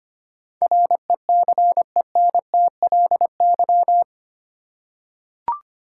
Morse code, heard as a keyed tone, about 25 wpm, 700 Hz. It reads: RECENTLY
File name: Eight Letter Words - 25wpm